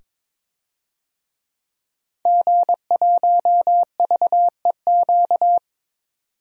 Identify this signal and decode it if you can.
Morse code, heard as a keyed tone, about 21 wpm, 700 Hz. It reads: G1VEQ
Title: DXCC-100 Call Signs - 5 Characters - 22wpm